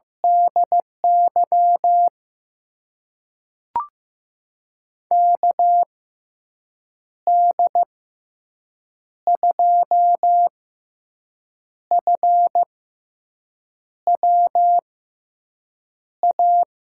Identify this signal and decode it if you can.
Morse code, heard as a keyed tone, about 15 wpm, spaced as for 6 wpm, 700 Hz. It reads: DY KD2FWA
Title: USA Call Signs - 6 Characters - 6x Character Spacing - 15wpm